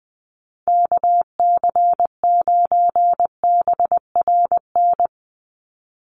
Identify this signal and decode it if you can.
Morse code, heard as a keyed tone, about 20 wpm, 700 Hz. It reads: KC9BRN
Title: USA Call Signs 20wpm